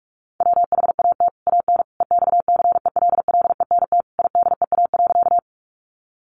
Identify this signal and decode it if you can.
Morse code, heard as a keyed tone, about 44 wpm, 700 Hz. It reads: WHAT AN EXCELLENT IDEA.
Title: Sentences from Top 1000 Words 45wpm